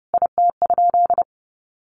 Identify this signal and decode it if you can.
Morse code, heard as a keyed tone, about 30 wpm, 700 Hz. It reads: IT?
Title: Sentences from Top 1000 Words 30wpm